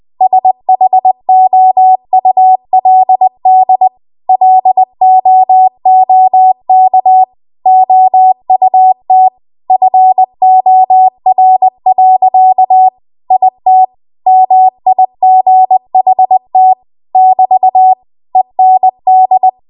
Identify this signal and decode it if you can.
Morse code, heard as a keyed tone, about 20 wpm, 750 Hz. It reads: SHOULD LOOK OUT FOR. IT MIGHT <BT> END
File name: Archive / 20 WPM / 20WPM